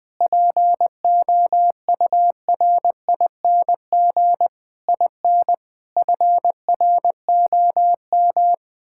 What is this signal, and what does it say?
Morse code, heard as a keyed tone, about 20 wpm, 700 Hz. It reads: POURING IN FROM
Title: Sets of 9 Words - 20wpm